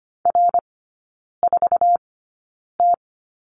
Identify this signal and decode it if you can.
Morse code, heard as a keyed tone, about 25 wpm, spaced as for 9 wpm, 690 Hz. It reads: R4T
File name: Lesson 14e - Number 2 - Three Characters 25wpm